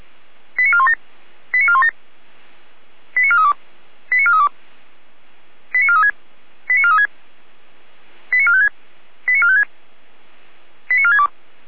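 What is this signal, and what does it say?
Five-tone selective call (ZVEI), heard as quick runs of five tones, tones of about 70 ms. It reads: 89417 89417 89432 89432 89436 89436 89456 89456 89462
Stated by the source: ZVEI five-tone selective call received on an SDR.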